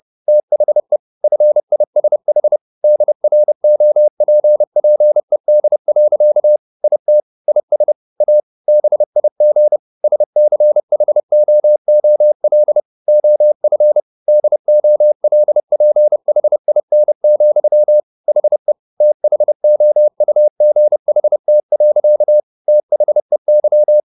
Morse code, heard as a keyed tone, about 30 wpm, 600 Hz. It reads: THE FISH DROPPED. IT IS A BIG SCHOOL OF DOLPHIN, HE THOUGHT. THEY
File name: the old man and the sea / the old man and the sea 1 30wpm